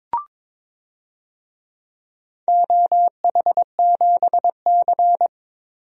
Morse code, heard as a keyed tone, about 21 wpm, 700 Hz. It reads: OH7C
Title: DXCC-100 Call Signs - 4 Characters - Repeated 2x - 22wpm